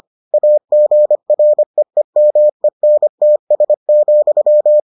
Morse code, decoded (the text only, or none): AGREEMENTS,